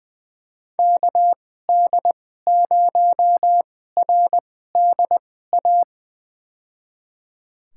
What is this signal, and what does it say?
Morse code, heard as a keyed tone, about 20 wpm, spaced as for 15 wpm, 700 Hz. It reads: KD0RDA
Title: USA Call Signs - 2x Character Spacing - 20wpm